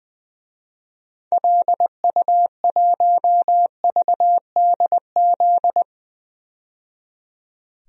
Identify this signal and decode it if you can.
Morse code, heard as a keyed tone, about 20 wpm, 700 Hz. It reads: LU1VDZ
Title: DXCC-100 Call Signs - 6 Characters - 20wpm